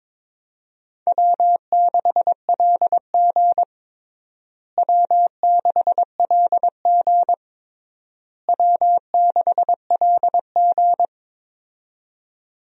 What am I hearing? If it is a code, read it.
Morse code, heard as a keyed tone, about 22 wpm, 700 Hz. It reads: W6LG W6LG W6LG
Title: USA Call Signs - 4 Characters - Repeated 3x - 22wpm